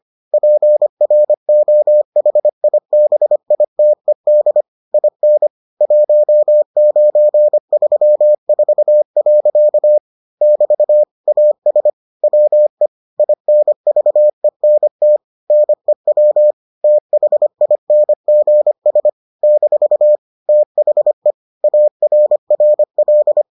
Morse code, heard as a keyed tone, about 25 wpm, 600 Hz. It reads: PROHIBITED IN 1934. <BT> AS WE INVENT NEW THINGS - THE ARRL